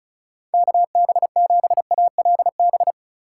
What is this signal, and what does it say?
Morse code, heard as a keyed tone, about 34 wpm, 690 Hz. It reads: KB7ALB
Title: USA Call Signs - 6 Characters - 2x Character Spacing - 35wpm